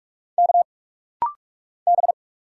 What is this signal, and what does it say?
Morse code, heard as a keyed tone, about 46 wpm, 700 Hz. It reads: K B
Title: Single Letter-Number Alt - Rapid-Fire 45wpm